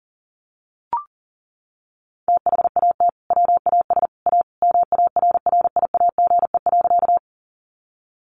Morse code, heard as a keyed tone, about 40 wpm, 700 Hz. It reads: THAT WAS A MARRIAGE.
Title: Sentences from Top 800 Words 40wpm